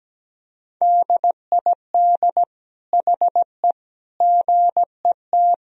Morse code, heard as a keyed tone, about 17 wpm, 700 Hz. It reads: DID HE GET
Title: Sentences from Top 1000 Words - Review Part 4 - 17wpm